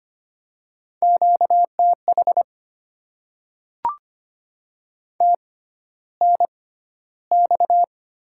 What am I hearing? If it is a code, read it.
Morse code, heard as a keyed tone, about 25 wpm, spaced as for 13 wpm, 700 Hz. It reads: QTH TNX
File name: Top 20 QSO Elements - 6x Character Spacing - 25wpm